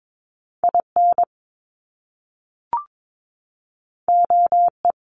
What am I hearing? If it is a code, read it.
Morse code, heard as a keyed tone, about 22 wpm, 700 Hz. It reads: IN OE
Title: Top Two Letter Combinations in Words 22wpm